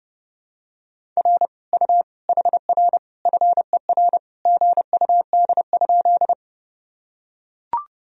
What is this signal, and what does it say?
Morse code, heard as a keyed tone, about 30 wpm, 700 Hz. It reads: R U HR FER GUD?
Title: Sentences from Top 200 Words and Abbreviations 30wpm